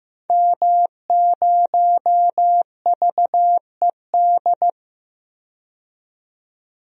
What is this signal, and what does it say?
Morse code, heard as a keyed tone, about 15 wpm, 700 Hz. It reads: M0VED
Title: DXCC-100 Call Signs - 5 Characters - 15wpm